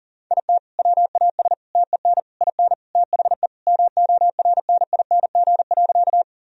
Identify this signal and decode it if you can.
Morse code, heard as a keyed tone, about 40 wpm, 700 Hz. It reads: IT WAS TEN IN THE MORNING.
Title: Sentences from Top 1000 Words - Review Part 2 - 40wpm